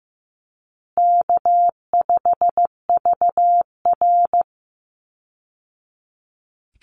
Morse code, heard as a keyed tone, about 15 wpm, 700 Hz. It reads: K5VR
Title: Call Signs Repeated 2x - 15wpm